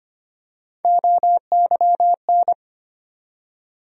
Morse code, heard as a keyed tone, about 25 wpm, 700 Hz. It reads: OYN
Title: USA Call Sign Postfixes - 3 Letters 25wpm